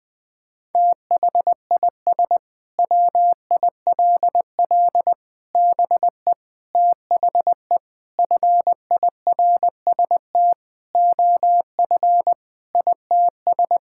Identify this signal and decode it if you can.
Morse code, heard as a keyed tone, about 20 wpm, 700 Hz. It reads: THIS WILL BE THE FIRST OF ITS
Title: Sentences with 8-Word Phrases - Spoken Beginning - 20wpm